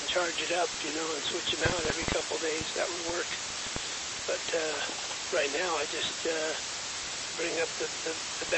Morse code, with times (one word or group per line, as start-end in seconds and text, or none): none